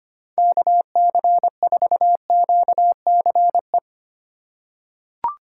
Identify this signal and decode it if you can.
Morse code, heard as a keyed tone, about 25 wpm, 700 Hz. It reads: KC4QCE
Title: USA Call Signs - 6 Characters - 3x Character Spacing - 25wpm